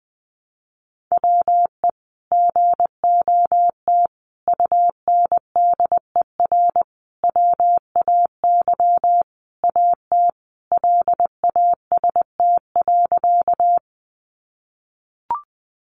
Morse code, heard as a keyed tone, about 20 wpm, 700 Hz. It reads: WE GOT UNDER WAY AT LAST.
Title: Sentences from Top 300 Words 20wpm